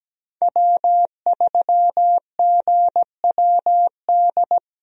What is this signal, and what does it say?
Morse code, heard as a keyed tone, about 17 wpm, 700 Hz. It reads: W3GWD
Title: USA Call Signs - 5 Characters - Repeated 2x - 17wpm